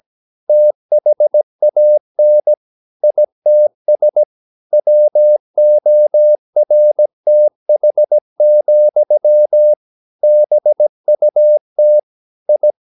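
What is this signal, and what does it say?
Morse code, heard as a keyed tone, about 17 wpm, 600 Hz. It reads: THAN ITS WORTH, BUT I
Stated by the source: 17 wpm Morse practice sending the English text of a novel.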